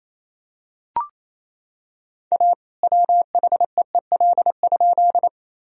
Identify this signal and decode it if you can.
Morse code, heard as a keyed tone, about 28 wpm, 700 Hz. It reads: A WHEEL?
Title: Sentences from Top 1000 Words - Review Part 3 - 28wpm